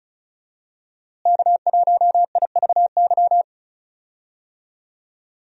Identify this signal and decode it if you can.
Morse code, heard as a keyed tone, about 34 wpm, 700 Hz. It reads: K1IVY
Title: USA Call Signs - 5 Characters - Repeated 2x - 35wpm